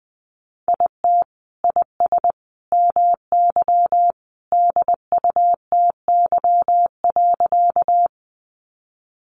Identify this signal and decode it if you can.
Morse code, heard as a keyed tone, about 19 wpm, 700 Hz. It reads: IT IS MY DUTY.
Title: Sentences from Top 1500 Words 20wpm